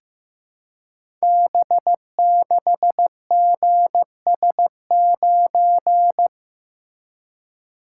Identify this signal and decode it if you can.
Morse code, heard as a keyed tone, about 15 wpm, 700 Hz. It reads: B6GS9
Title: Code Groups - 5 Letter:Numbers - 15wpm